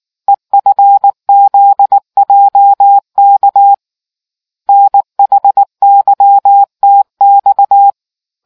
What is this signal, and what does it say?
Morse code, heard as a keyed tone, about 19 wpm, 800 Hz. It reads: EFZJK NHYTX